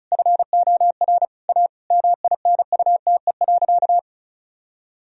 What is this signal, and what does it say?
Morse code, heard as a keyed tone, about 36 wpm, 700 Hz. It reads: FOR A MINUTE.
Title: Sentences from Top 1000 Words - Review Part 2 - 35wpm